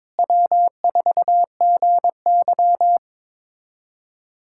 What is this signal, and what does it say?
Morse code, heard as a keyed tone, about 21 wpm, 700 Hz. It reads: W4GY
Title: USA Callsign DE Callsign - 4 Characters - 22wpm